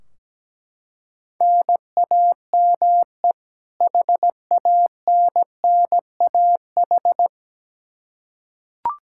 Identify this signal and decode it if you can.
Morse code, heard as a keyed tone, about 17 wpm, 700 Hz. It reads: NAME HANNAH
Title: Practice QSO - Exchange 1 - Beginning - 17wpm